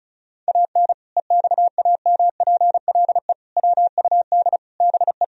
Morse code, heard as a keyed tone, about 34 wpm, 700 Hz. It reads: AN EXAMPLE WUD BE
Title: Sentences from Top 400 Words and Abbreviations 35wpm